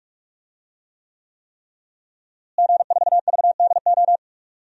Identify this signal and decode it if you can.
Morse code, heard as a keyed tone, about 46 wpm, 700 Hz. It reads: G4VDO
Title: DXCC-100 Call Signs - 5 Characters - Repeated 3x - 45wpm